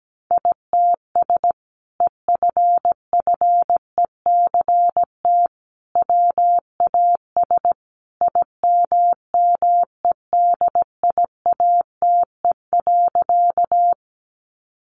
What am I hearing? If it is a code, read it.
Morse code, heard as a keyed tone, about 17 wpm, 700 Hz. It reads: ITS EFFECT WAS IMMEDIATE.
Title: Sentences from Top 1200 Words 17wpm